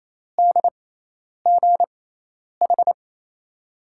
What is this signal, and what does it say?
Morse code, heard as a keyed tone, about 28 wpm, spaced as for 10 wpm, 700 Hz. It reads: DGH
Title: USA Call Signs - 6 Characters - 6x Character Spacing - 28wpm